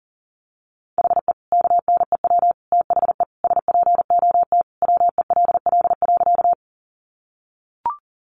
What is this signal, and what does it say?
Morse code, heard as a keyed tone, about 40 wpm, 700 Hz. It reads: HE KNEW THE SPOT WELL.
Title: Sentences from Top 1100 Words 40wpm